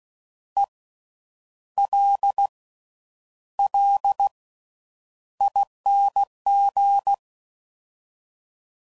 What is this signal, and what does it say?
Morse code, heard as a keyed tone, about 16 wpm, spaced as for 8 wpm, 800 Hz. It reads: ELLING
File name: Slow With Faster Ings